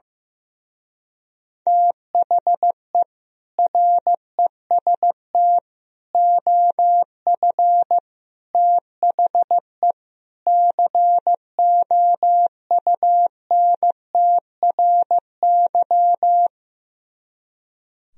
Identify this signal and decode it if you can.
Morse code, heard as a keyed tone, about 15 wpm, 700 Hz. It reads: THE REST OF THE COUNTRY
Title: Speed-Racing - Sets of 5 Words 15wpm